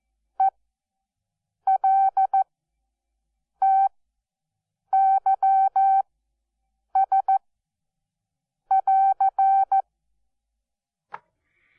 Morse code, heard as a keyed tone, about 14 wpm, spaced as for 6 wpm, 770 Hz. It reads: ELTYS<AR>
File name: Onlinelehrgang / morsen / DJ4UF Morsen12